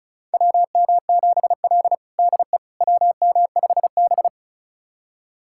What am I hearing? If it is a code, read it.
Morse code, heard as a keyed tone, about 36 wpm, 700 Hz. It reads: WM7L DE WM5B
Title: USA Callsign DE Callsign - 4 Characters - 35wpm